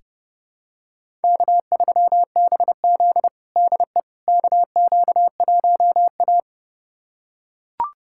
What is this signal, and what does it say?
Morse code, heard as a keyed tone, about 30 wpm, 700 Hz. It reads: K3BZ DE KQ1A
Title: USA Callsign DE Callsign - 4 Characters - 30wpm